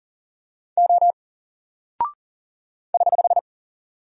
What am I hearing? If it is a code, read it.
Morse code, heard as a keyed tone, about 39 wpm, 700 Hz. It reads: O <HH>
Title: Single Character (Letter:Number:Punctuation:Prosign) Alt 40wpm